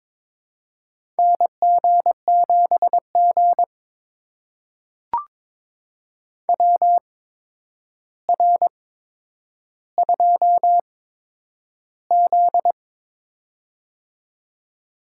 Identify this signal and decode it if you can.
Morse code, heard as a keyed tone, about 22 wpm, spaced as for 9 wpm, 700 Hz. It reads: NG7G WR2Z
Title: USA Call Signs - 4 Characters - 8x Character Spacing - 22wpm